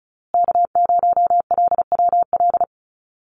Morse code, heard as a keyed tone, about 34 wpm, 700 Hz. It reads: K0LWL
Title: USA Call Signs Repeated 3x 35wpm